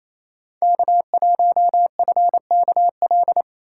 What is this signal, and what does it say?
Morse code, heard as a keyed tone, about 33 wpm, 700 Hz. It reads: K1FKL
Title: Speed Racing - USA 1x3 Call Signs 28wpm